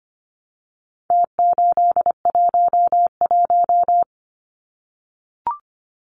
Morse code, heard as a keyed tone, about 25 wpm, 700 Hz. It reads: T811
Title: Numbers - Contest Style - 4-Digit 25wpm